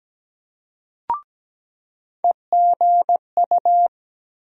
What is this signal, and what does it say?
Morse code, heard as a keyed tone, about 17 wpm, 700 Hz. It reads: EGU
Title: USA Call Sign Postfixes - 3 Letters 17wpm